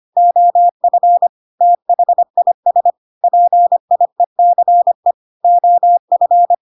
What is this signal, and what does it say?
Morse code, heard as a keyed tone, about 25 wpm, 700 Hz. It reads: OF THIS PIECE OF
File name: Pinocchio / Pinocchio ch2 25 25 ik1hge